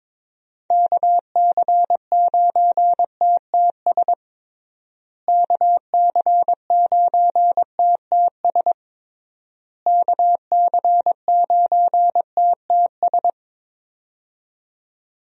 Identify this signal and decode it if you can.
Morse code, heard as a keyed tone, about 22 wpm, 700 Hz. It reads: KC9TTS KC9TTS KC9TTS
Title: USA Call Signs - 6 Characters - Repeated 3x - 22wpm